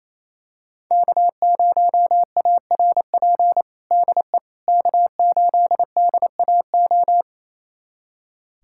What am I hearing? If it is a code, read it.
Morse code, heard as a keyed tone, about 28 wpm, 700 Hz. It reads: K0ARP DE K8DAO
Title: USA Callsign DE Callsign - 5 Characters - 28wpm